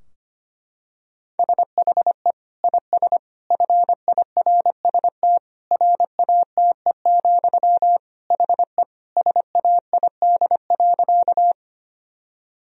Morse code, heard as a keyed tone, about 25 wpm, 700 Hz. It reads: SHE IS FIRST RATE, HE SAID.
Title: Sentences from Top 1000 Words - Review Part 4 - 25wpm